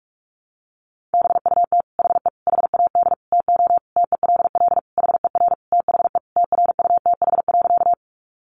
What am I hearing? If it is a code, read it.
Morse code, heard as a keyed tone, about 44 wpm, 700 Hz. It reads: BUT HE HAD TO TELL HER THE TRUTH.
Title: Sentences from Top 800 Words 45wpm